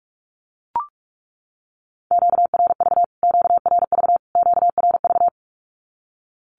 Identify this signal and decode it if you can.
Morse code, heard as a keyed tone, about 44 wpm, 700 Hz. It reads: QRV QRV QRV
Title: Q-Codes Repeated 3x 45wpm